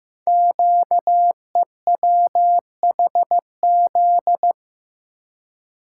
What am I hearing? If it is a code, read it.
Morse code, heard as a keyed tone, about 15 wpm, 700 Hz. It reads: QEWHZ
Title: Code Groups - 5 Letter:Numbers - 15wpm